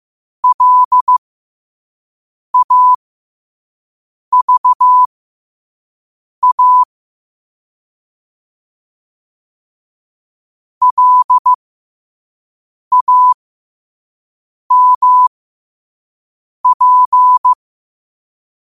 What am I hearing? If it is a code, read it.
Morse code, heard as a keyed tone, about 15 wpm, spaced as for 5 wpm, 1000 Hz. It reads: LAVA LAMP
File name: qotdfiles / 5wpm